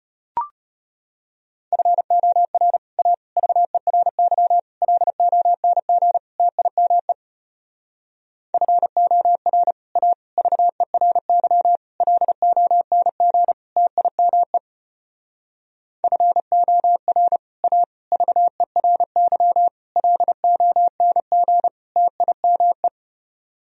Morse code, heard as a keyed tone, about 32 wpm, 700 Hz. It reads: FOR A VERY LONG TIME FOR A VERY LONG TIME FOR A VERY LONG TIME